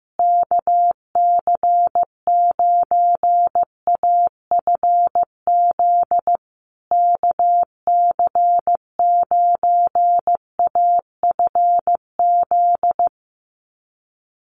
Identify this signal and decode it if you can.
Morse code, heard as a keyed tone, about 15 wpm, 700 Hz. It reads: KC9AFZ KC9AFZ
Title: USA Call Signs Repeated 3x 15wpm